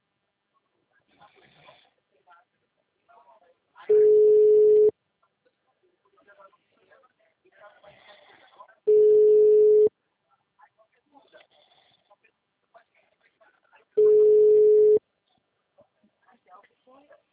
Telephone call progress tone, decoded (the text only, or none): ringback tone